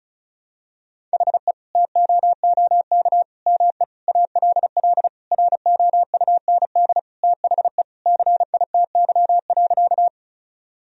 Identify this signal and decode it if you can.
Morse code, heard as a keyed tone, about 34 wpm, 700 Hz. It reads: HE TOOK ME ALL ROUND THE CITY.
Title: Sentences from Top 1000 Words - Review Part 1 - 35wpm